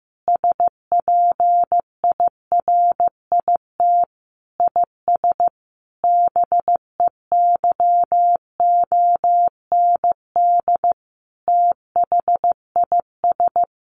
Morse code, heard as a keyed tone, about 14 wpm, 700 Hz. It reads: SPIRIT IS BEYOND THIS
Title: Sentences from Top 1500 Words 15wpm